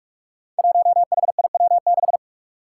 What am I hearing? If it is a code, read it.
Morse code, heard as a keyed tone, about 46 wpm, 700 Hz. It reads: 1HIW6
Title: Five Challenging Letter-Number Combinations 45wpm